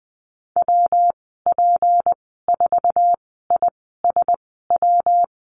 Morse code, heard as a keyed tone, about 20 wpm, spaced as for 14 wpm, 700 Hz. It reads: WP4ISW
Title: 179 Lesson 29g - Slash - Call Signs 20wpm